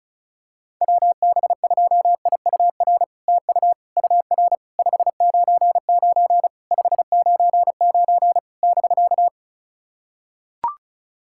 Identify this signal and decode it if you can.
Morse code, heard as a keyed tone, about 36 wpm, 700 Hz. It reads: WB2IUR TU UR 599 599 <BK>
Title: SOTA QSO Practice 35wpm